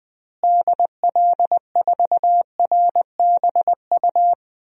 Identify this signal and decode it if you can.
Morse code, heard as a keyed tone, about 21 wpm, 700 Hz. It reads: DL4RBU
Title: DXCC-100 Call Signs - 6 Characters - Repeated 2x - 20wpm